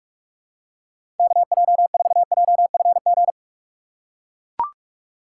Morse code, heard as a keyed tone, about 44 wpm, 700 Hz. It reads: KJ4JFG
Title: USA Call Signs - 2x Character Spacing - 45wpm